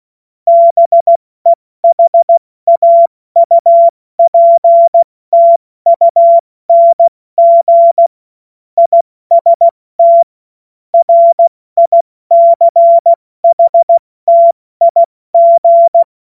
Morse code, heard as a keyed tone, about 16 wpm, spaced as for 14 wpm, 680 Hz. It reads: BEHAUPTUNG IST RICHTIG
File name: Morsekurs / Witze / Witz 12. April 2022 70 F80